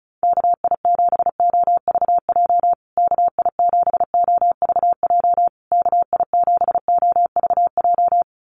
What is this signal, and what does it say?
Morse code, heard as a keyed tone, about 34 wpm, 700 Hz. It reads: KI7OVJ KI7OVJ KI7OVJ
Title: USA Call Signs Repeated 3x 35wpm